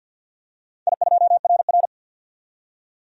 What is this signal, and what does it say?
Morse code, heard as a keyed tone, about 47 wpm, 700 Hz. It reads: I2RR
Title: DXCC-100 Call Signs - 4 Characters - 50wpm